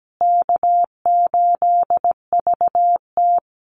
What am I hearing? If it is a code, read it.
Morse code, heard as a keyed tone, about 17 wpm, 700 Hz. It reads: K8VT
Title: USA Call Signs Repeated 2x 17wpm